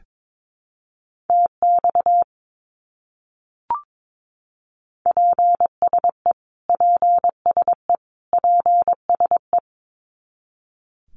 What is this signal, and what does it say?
Morse code, heard as a keyed tone, about 22 wpm, 700 Hz. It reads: TX PSE PSE PSE
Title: Abbreviations Repeated 3x - 22wpm